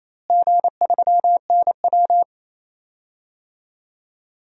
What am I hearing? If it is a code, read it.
Morse code, heard as a keyed tone, about 28 wpm, 700 Hz. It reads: G3NW
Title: DXCC-100 Call Signs - 4 Characters - Repeated 3x - 28wpm